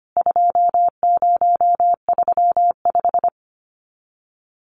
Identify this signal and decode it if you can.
Morse code, heard as a keyed tone, about 25 wpm, 690 Hz. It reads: 2035
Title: CWT Exchange 25wpm